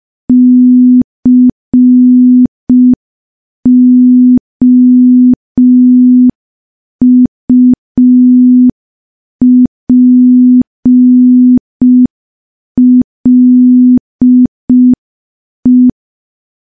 Morse code, heard as a keyed tone, about 5 wpm, 250 Hz. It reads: COUPLE
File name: morse / morse-training-05WPM